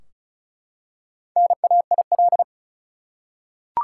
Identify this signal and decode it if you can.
Morse code, heard as a keyed tone, about 36 wpm, 700 Hz. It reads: NAIL
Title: Four Letter Words - 35wpm